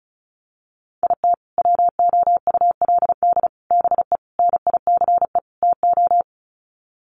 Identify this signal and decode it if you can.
Morse code, heard as a keyed tone, about 34 wpm, 700 Hz. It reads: IT WOULD BE NICE TO
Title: Sets of 5 Words Encore2 35wpm